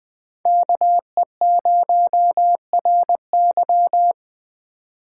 Morse code, heard as a keyed tone, about 24 wpm, 700 Hz. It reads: KE0RY
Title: Speed Racing - USA Call Signs 20wpm